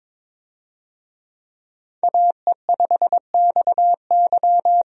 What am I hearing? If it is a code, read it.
Morse code, heard as a keyed tone, about 22 wpm, 700 Hz. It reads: AE5XY
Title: USA Call Signs - 5 Characters - 22wpm